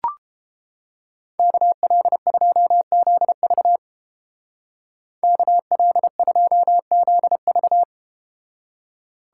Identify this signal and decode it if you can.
Morse code, heard as a keyed tone, about 30 wpm, 700 Hz. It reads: KL2ZV KL2ZV